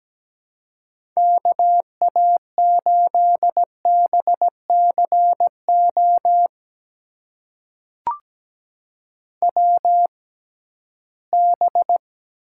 Morse code, heard as a keyed tone, about 17 wpm, spaced as for 13 wpm, 700 Hz. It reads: KA8BCO WB
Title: USA Call Signs - 6 Characters - 6x Character Spacing - 17wpm